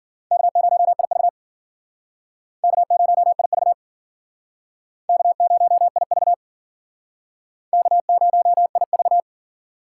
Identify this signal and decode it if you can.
Morse code, heard as a keyed tone, about 50 wpm, 700 Hz. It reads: K0IV K0IV K0IV K0IV
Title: Speed Racing - USA 1x2 Call Signs 40wpm